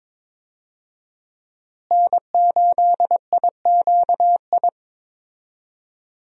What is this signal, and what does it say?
Morse code, heard as a keyed tone, about 23 wpm, 700 Hz. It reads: N8IQI
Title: USA Call Signs - 5 Characters - Repeated 2x - 22wpm